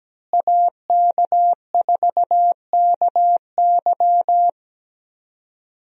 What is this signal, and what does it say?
Morse code, heard as a keyed tone, about 17 wpm, 700 Hz. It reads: AK4KY
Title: USA Call Signs - 5 Characters - Repeated 3x - 17wpm